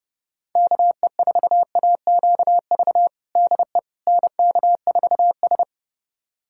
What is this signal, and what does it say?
Morse code, heard as a keyed tone, about 30 wpm, 700 Hz. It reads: KE4AQV DE NK4S
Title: USA Callsign DE Callsign 30wpm